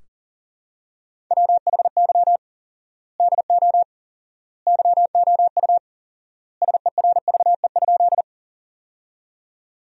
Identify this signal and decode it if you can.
Morse code, heard as a keyed tone, about 40 wpm, 700 Hz. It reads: WHY DO YOU SERVE?
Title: 4x Word-Spacing - Sentences from Top 400 Words - 40wpm